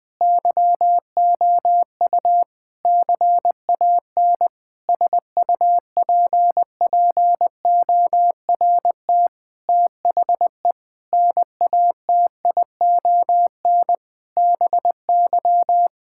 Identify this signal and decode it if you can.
Morse code, heard as a keyed tone, about 20 wpm, 700 Hz. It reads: YOU CAN SUPPORT THE NATION BY
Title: Sentences with 9-Word Phrases - 20wpm